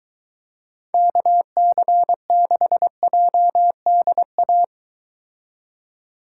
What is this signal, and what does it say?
Morse code, heard as a keyed tone, about 23 wpm, 700 Hz. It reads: KC6JDA